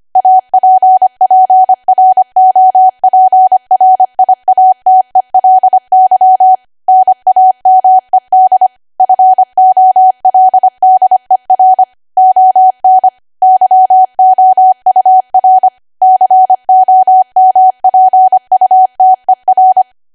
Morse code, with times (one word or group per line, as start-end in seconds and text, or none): 0.2-6.5 APPROPRIATELY
6.9-8.7 NAMED
9.0-11.8 FOLDER
12.2-13.1 ON
13.4-15.7 YOUR
16.0-19.8 COMPUTER